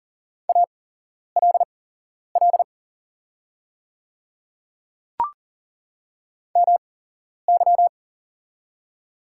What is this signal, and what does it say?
Morse code, heard as a keyed tone, about 40 wpm, spaced as for 11 wpm, 690 Hz. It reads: ALL MY
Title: 8x Character Spacing - Top 100 Words - 40wpm